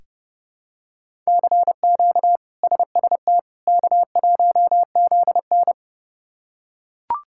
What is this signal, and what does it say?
Morse code, heard as a keyed tone, about 30 wpm, 700 Hz. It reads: CQ SST K1ZN
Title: K1USN SST QSO Practice - 30wpm